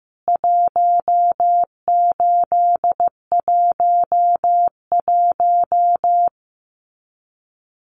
Morse code, heard as a keyed tone, about 15 wpm, 700 Hz. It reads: 1811
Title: CWT Exchange 15wpm